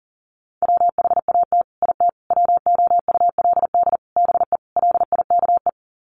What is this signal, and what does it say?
Morse code, heard as a keyed tone, about 40 wpm, 700 Hz. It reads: WHAT IT WOULD BE LIKE
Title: Sets of 5 Words Encore 40wpm